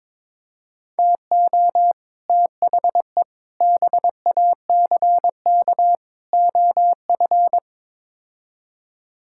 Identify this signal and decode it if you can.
Morse code, heard as a keyed tone, about 22 wpm, 700 Hz. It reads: TO THE BACK OF
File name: practice / 22wpm / Build A Phrase - 5-Word Phrases - 22wpm